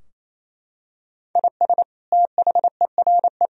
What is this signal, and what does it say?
Morse code, heard as a keyed tone, about 28 wpm, 700 Hz. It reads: IS THERE
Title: Sentences from Top 1000 Words - Review Part 4 - 28wpm